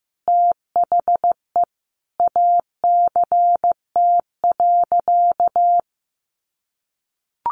Morse code, heard as a keyed tone, about 15 wpm, 700 Hz. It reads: THE ACT.
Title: Sentences from Top 1300 Words 15wpm